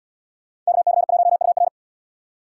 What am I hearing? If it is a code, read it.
Morse code, heard as a keyed tone, about 75 wpm, 700 Hz. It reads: KC1UR